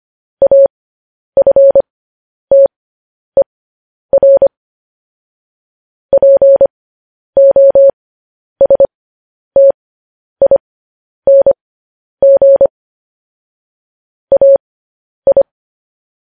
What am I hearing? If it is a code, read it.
Morse code, heard as a keyed tone, about 25 wpm, spaced as for 10 wpm, 550 Hz. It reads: AFTER POSTING AI
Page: News Headlines in Morse Code at 10 WPM